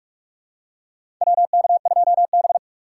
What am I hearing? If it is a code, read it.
Morse code, heard as a keyed tone, about 46 wpm, 700 Hz. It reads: WK2B
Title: USA Call Signs - 4 Characters - 6x Character Spacing - 45wpm